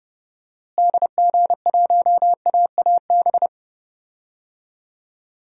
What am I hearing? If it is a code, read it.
Morse code, heard as a keyed tone, about 30 wpm, 700 Hz. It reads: DG1AAB
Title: DXCC-100 Call Signs - 6 Characters - Repeated 2x - 30wpm